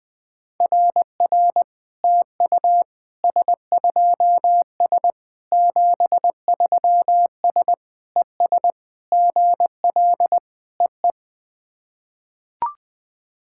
Morse code, heard as a keyed tone, about 20 wpm, 690 Hz. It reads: RR TU S2S 73S ES GL EE
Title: SOTA QSO Practice 20wpm